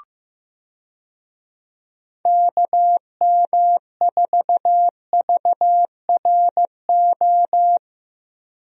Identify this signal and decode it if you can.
Morse code, heard as a keyed tone, about 15 wpm, 700 Hz. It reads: KM4VRO
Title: USA Call Signs - 6 Characters - Repeated 3x - 15wpm